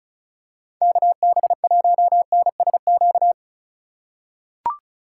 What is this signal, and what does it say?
Morse code, heard as a keyed tone, about 34 wpm, 700 Hz. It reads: KB1NSQ
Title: USA Call Signs - 4x Character Spacing - 35wpm